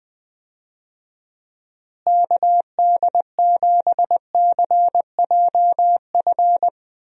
Morse code, heard as a keyed tone, about 20 wpm, 700 Hz. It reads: KD7CJF
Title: USA Call Signs - 6 Characters - Repeated 3x - 20wpm